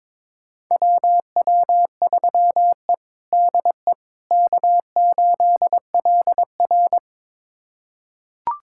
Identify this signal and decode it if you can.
Morse code, heard as a keyed tone, about 22 wpm, 700 Hz. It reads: WW3E DE K8LR
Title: USA Callsign DE Callsign - 4 Characters - 22wpm